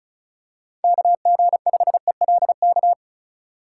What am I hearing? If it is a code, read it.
Morse code, heard as a keyed tone, about 36 wpm, 700 Hz. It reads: KG5ELK
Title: USA Call Signs - 3x Character Spacing - 35wpm